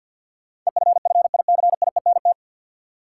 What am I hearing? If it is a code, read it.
Morse code, heard as a keyed tone, about 53 wpm, 700 Hz. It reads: EFFICIENT
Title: Sentences from Top 1300 Words - New Word Drill Alt - 50wpm